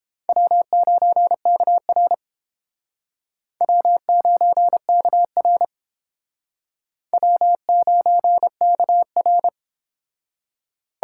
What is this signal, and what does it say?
Morse code, heard as a keyed tone, about 30 wpm, 700 Hz. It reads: W9KR W9KR W9KR